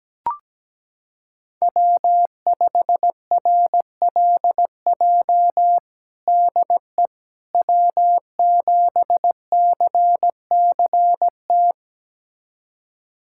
Morse code, heard as a keyed tone, about 17 wpm, 700 Hz. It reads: W5RLJ DE W7CCT
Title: USA Callsign DE Callsign - 5 Characters - 17wpm